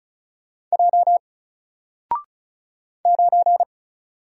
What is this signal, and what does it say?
Morse code, heard as a keyed tone, about 36 wpm, 700 Hz. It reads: J 9
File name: Single-Letter-Number-35wpm